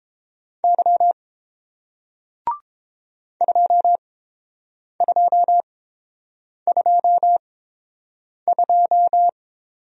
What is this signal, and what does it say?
Morse code, heard as a keyed tone, about 30 wpm, spaced as for 7 wpm, 700 Hz. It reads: Y 2222